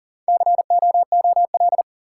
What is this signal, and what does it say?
Morse code, heard as a keyed tone, about 40 wpm, 700 Hz. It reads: COOL
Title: Sentences from Top 1200 Words - New Word Drill - 40wpm